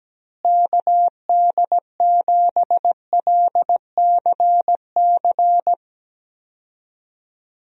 Morse code, heard as a keyed tone, about 17 wpm, 700 Hz. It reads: KD7LCC
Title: USA Call Signs - 6 Characters - Repeated 2x - 17wpm